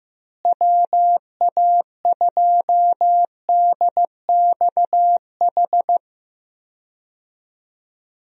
Morse code, heard as a keyed tone, about 15 wpm, 700 Hz. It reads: WA2DXH
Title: USA Call Signs - 6 Characters - Repeated 2x - 15wpm